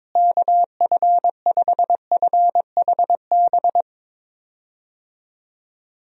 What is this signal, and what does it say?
Morse code, heard as a keyed tone, about 22 wpm, 700 Hz. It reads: KF5FHB
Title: USA Call Signs - 6 Characters - Repeated 3x - 22wpm